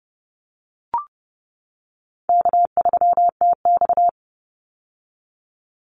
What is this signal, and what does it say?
Morse code, heard as a keyed tone, about 30 wpm, 700 Hz. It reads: K3TX
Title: Call Signs - 30wpm